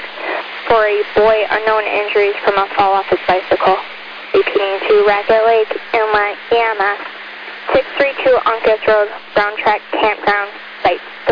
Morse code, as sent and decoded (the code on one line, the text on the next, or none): none